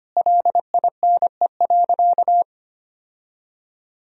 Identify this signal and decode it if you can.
Morse code, heard as a keyed tone, about 25 wpm, 700 Hz. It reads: LINE.
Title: Sentences from Top 1000 Words - Review Part 1 - 25wpm